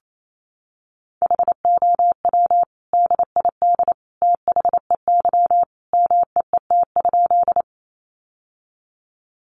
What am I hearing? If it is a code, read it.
Morse code, heard as a keyed tone, about 28 wpm, 700 Hz. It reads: HOW DID THEY MEET?
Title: Sentences from Top 1100 Words 28wpm